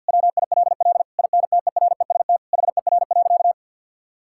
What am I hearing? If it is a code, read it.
Morse code, heard as a keyed tone, about 50 wpm, 700 Hz. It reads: WILL INTEREST HER.
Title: Sentences from Top 1000 Words - Review Part 5 - 50wpm